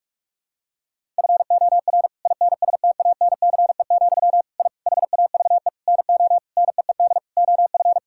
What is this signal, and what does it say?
Morse code, heard as a keyed tone, about 42 wpm, 700 Hz. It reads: FOR INSTANCE, I HAVE NO NEED OF
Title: Sentences from Top 1000 Words - Review Part 5 - 45wpm